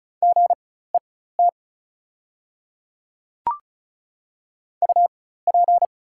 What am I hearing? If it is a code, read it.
Morse code, heard as a keyed tone, about 36 wpm, spaced as for 16 wpm, 700 Hz. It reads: GET UP
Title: 4x Character Spacing - Top 200 Words - 35wpm